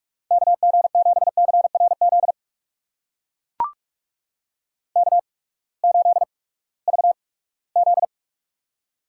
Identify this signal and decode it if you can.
Morse code, heard as a keyed tone, about 44 wpm, spaced as for 22 wpm, 700 Hz. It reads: KG7CRZ K8VZ
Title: USA Call Signs - 8x Character Spacing - 45wpm